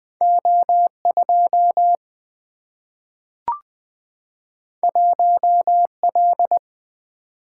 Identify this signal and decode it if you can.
Morse code, heard as a keyed tone, about 20 wpm, 700 Hz. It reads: O 2 1 L
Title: Single-Letter Single-Number 20wpm